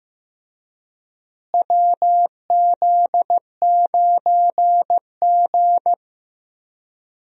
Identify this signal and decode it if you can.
Morse code, heard as a keyed tone, about 15 wpm, 700 Hz. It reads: WZ9G
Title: USA Call Signs - 4 Characters - Repeated 3x - 15wpm